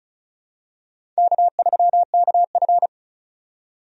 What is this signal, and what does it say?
Morse code, heard as a keyed tone, about 34 wpm, 700 Hz. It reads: K3KF
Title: USA Call Signs - 4 Characters - 35wpm